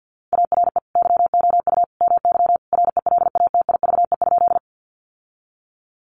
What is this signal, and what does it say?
Morse code, heard as a keyed tone, about 50 wpm, 700 Hz. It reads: ARE YOU MY RELATIVE?
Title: Sentences from Top 1400 Words 50wpm